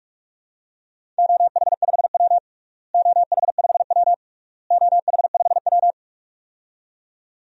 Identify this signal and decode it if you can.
Morse code, heard as a keyed tone, about 46 wpm, 700 Hz. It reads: OH5W OH5W OH5W
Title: DXCC-100 Call Signs - 4 Characters - Repeated 3x - 45wpm